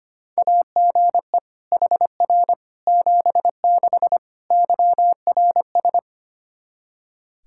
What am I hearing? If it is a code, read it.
Morse code, heard as a keyed tone, about 25 wpm, 700 Hz. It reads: AGE HR 76 YRS
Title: Practice QSO - Exchange 3 - More Info - 25wpm